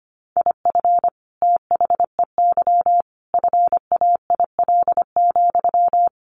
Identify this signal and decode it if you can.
Morse code, heard as a keyed tone, about 25 wpm, 700 Hz. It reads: IF THEY FAIL,
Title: Sentences from Top 1100 Words 25wpm